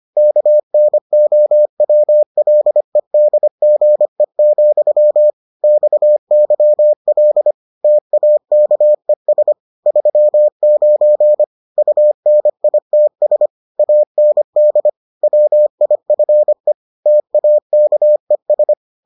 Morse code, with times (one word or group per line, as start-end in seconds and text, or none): 0.2-5.3 KNOWLEDGE,
5.6-7.5 XYL
7.8-9.5 TAKES
9.9-11.4 39
11.8-13.5 UNITS
13.8-14.9 AND
15.2-16.7 WIFE
17.1-18.7 TAKES